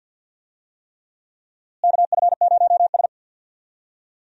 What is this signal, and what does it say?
Morse code, heard as a keyed tone, about 52 wpm, 700 Hz. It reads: KF0S
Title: USA Call Signs - 4 Characters - Repeated 3x - 50wpm